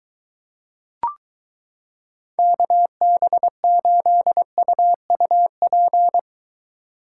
Morse code, heard as a keyed tone, about 23 wpm, 700 Hz. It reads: KB8UUP